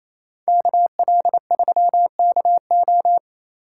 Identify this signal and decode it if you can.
Morse code, heard as a keyed tone, about 33 wpm, 700 Hz. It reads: KL3KO
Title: Speed Racing - USA 2x2 Call Signs 28wpm